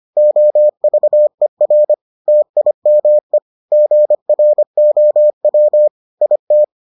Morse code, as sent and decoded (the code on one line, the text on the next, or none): --- ...- . .-. / - .. -- . / --. .-. --- .-- / .. -
OVER TIME GROW IT